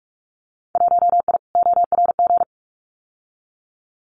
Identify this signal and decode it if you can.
Morse code, heard as a keyed tone, about 44 wpm, 700 Hz. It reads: 1I ORG
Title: Winter Field Day Exchanges 45wpm